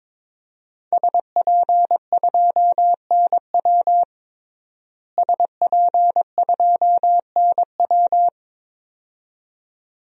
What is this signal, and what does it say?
Morse code, heard as a keyed tone, about 22 wpm, 700 Hz. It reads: SP2NW SP2NW
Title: DXCC-100 Call Signs - 5 Characters - Repeated 2x - 22wpm